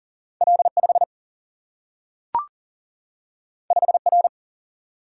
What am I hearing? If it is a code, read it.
Morse code, heard as a keyed tone, about 42 wpm, 700 Hz. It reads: L5 5R
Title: Single-Letter Single-Number 40wpm